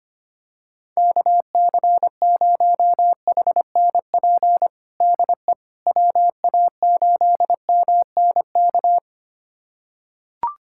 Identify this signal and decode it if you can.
Morse code, heard as a keyed tone, about 25 wpm, 700 Hz. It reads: KC0HNP DE WA8MNK
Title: USA Callsign DE Callsign 25wpm